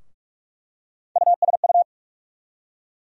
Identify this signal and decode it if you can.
Morse code, heard as a keyed tone, about 46 wpm, 700 Hz. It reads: USU
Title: USA Call Sign Postfixes - 3 Letters 45wpm